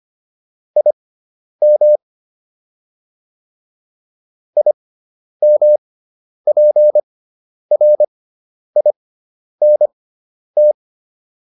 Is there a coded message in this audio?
Yes: Morse code from 0.8 to 10.7 s, heard as a keyed tone, about 25 wpm, spaced as for 10 wpm, 600 Hz. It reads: IM IMPRINT